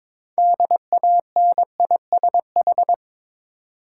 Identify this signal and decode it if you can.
Morse code, heard as a keyed tone, about 22 wpm, 690 Hz. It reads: DANISH
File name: Difficult Words with Hints - 22wpm